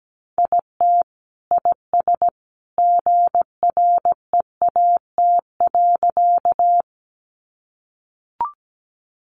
Morse code, heard as a keyed tone, about 17 wpm, 700 Hz. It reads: IT IS GREAT.
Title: Sentences from Top 200 Words 17wpm